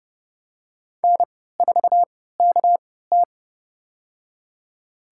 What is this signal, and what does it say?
Morse code, heard as a keyed tone, about 30 wpm, spaced as for 17 wpm, 700 Hz. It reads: N4KT
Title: USA Call Signs - 3x Character Spacing - 30wpm